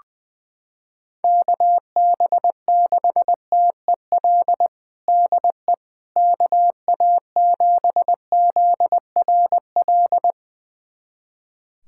Morse code, heard as a keyed tone, about 21 wpm, 700 Hz. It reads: KB6TEL DE KA7ZRL
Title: USA Callsign DE Callsign 20wpm